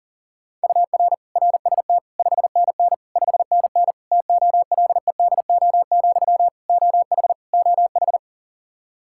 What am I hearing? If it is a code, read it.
Morse code, heard as a keyed tone, about 40 wpm, 700 Hz. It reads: UR RST 5NN 5NN TOLEDO, OH OH
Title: Practice QSO - Exchange 1 - Beginning - 40wpm